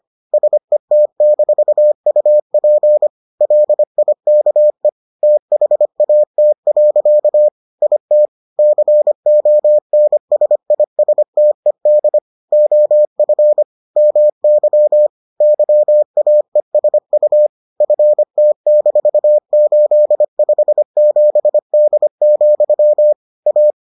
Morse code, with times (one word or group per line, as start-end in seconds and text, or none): 0.3-3.1 SET-UP
3.4-4.9 LIKE
5.2-7.5 THAT.
7.8-8.3 IT
8.6-12.2 CONSISTED
12.5-13.6 OF
14.0-15.1 MY
15.4-17.5 YAESU
17.8-23.1 FT-857D,
23.5-23.7 A